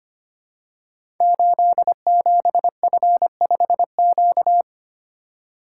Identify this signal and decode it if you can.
Morse code, heard as a keyed tone, about 25 wpm, 700 Hz. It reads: 87F5Q
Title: Code Groups - 5 Letter:Numbers - 25wpm